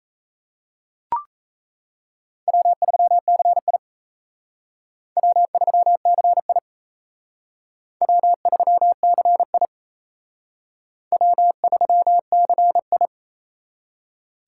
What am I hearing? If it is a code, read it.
Morse code, heard as a keyed tone, about 35 wpm, 700 Hz. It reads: W3CI W3CI W3CI W3CI
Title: Speed Racing - USA 1x2 Call Signs 28wpm